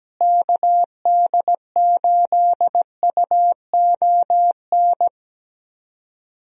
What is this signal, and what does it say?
Morse code, heard as a keyed tone, about 17 wpm, 700 Hz. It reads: KD8UON
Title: USA Call Signs - 6 Characters - Repeated 2x - 17wpm